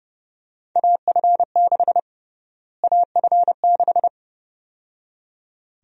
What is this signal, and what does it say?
Morse code, heard as a keyed tone, about 29 wpm, 700 Hz. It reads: AF6 AF6
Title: USA Call Sign Prefixes - Repeated 2x - 30wpm